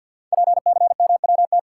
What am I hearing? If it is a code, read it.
Morse code, heard as a keyed tone, about 53 wpm, 700 Hz. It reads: PCMWT
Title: Code Groups - 5 Letters - 50wpm